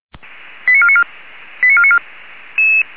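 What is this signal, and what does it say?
Five-tone selective call (ZVEI), heard as quick runs of five tones, tones of about 70 ms. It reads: 89494 89494
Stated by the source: ZVEI five-tone call 89155 received by SDR.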